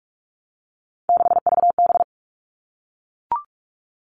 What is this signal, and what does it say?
Morse code, heard as a keyed tone, about 44 wpm, 700 Hz. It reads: 6VB
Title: All Cut Numbers - 3-Digit 45wpm